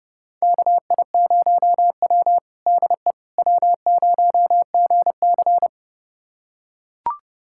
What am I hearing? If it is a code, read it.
Morse code, heard as a keyed tone, about 32 wpm, 700 Hz. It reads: KI0W DE W0GC
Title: USA Callsign DE Callsign - 4 Characters - 30wpm